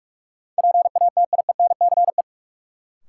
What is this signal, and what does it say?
Morse code, heard as a keyed tone, about 46 wpm, 700 Hz. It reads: PATIENCE
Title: Sentences from Top 1300 Words - New Word Drill Alt - 45wpm